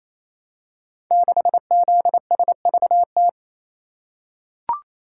Morse code, heard as a keyed tone, about 27 wpm, 700 Hz. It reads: 6ZSVT
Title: Code Groups - 5 Letter:Numbers - 28wpm